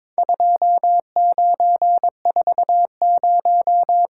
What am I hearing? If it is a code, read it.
Morse code, heard as a keyed tone, about 22 wpm, 700 Hz. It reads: 2940
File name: practice / 22wpm / Four Letter-Number Combinations - 22wpm